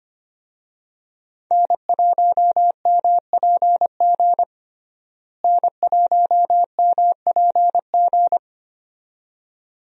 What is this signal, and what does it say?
Morse code, heard as a keyed tone, about 25 wpm, 700 Hz. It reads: N1MPG N1MPG
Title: USA Call Signs - 5 Characters - Repeated 2x - 25wpm